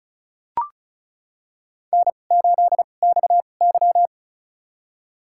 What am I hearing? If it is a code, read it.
Morse code, heard as a keyed tone, about 36 wpm, spaced as for 25 wpm, 700 Hz. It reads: N8XY
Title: USA Call Signs - 4 Characters - 2x Character Spacing - 35wpm